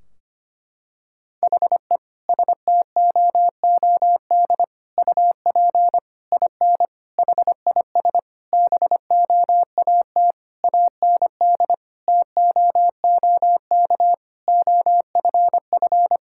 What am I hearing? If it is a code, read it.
Morse code, heard as a keyed tone, about 25 wpm, 700 Hz. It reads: HE STOOD UP IN HIS BOAT AND TOOK OFF
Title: Sentences from Top 1600 Words 25wpm